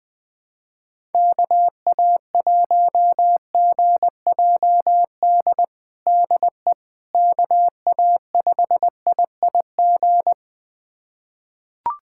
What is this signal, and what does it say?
Morse code, heard as a keyed tone, about 20 wpm, 700 Hz. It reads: KA1GJD DE KA5IIG
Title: USA Callsign DE Callsign - 6 Characters - 20wpm